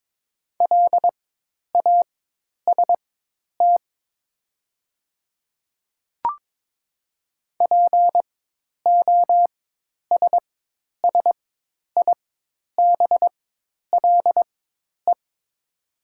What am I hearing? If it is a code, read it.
Morse code, heard as a keyed tone, about 22 wpm, spaced as for 10 wpm, 700 Hz. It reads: LAST POSSIBLE
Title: 4x Character Spacing - Top 500 Words - 22wpm